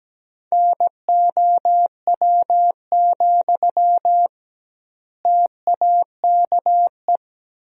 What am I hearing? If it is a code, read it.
Morse code, heard as a keyed tone, about 17 wpm, 700 Hz. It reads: NOW, TAKE
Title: 2x Word-Spacing - Sentences from Top 400 Words - 17wpm